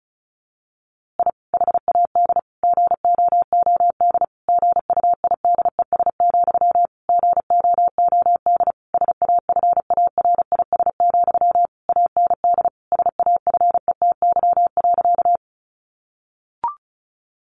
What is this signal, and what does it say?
Morse code, heard as a keyed tone, about 34 wpm, 700 Hz. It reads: I HAD GOOD GUIDES, GOOD SAFARIS, AND SAFETY.
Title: A Sound Of Thunder 35wpm